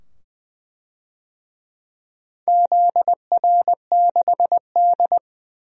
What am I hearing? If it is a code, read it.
Morse code, heard as a keyed tone, about 20 wpm, 700 Hz. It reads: ZR6D
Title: DXCC-100 Call Signs - 4 Characters - 20wpm